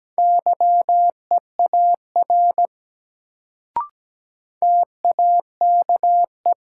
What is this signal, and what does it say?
Morse code, heard as a keyed tone, about 17 wpm, 700 Hz. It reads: YEAR TAKE
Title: Sentences from Top 200 Words - New Word Drill - 17wpm